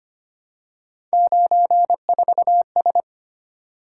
Morse code, heard as a keyed tone, about 31 wpm, 700 Hz. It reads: 94S
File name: Speed-Racing - Three Letter-Number Combinations - 25wpm